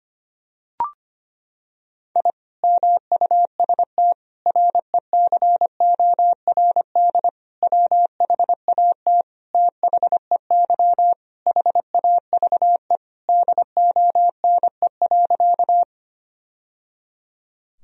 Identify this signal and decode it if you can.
Morse code, heard as a keyed tone, about 25 wpm, 700 Hz. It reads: I MUST RECORD WHAT THEY HAVE DONE.
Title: Sentences from Top 1000 Words - Review Part 3 - 25wpm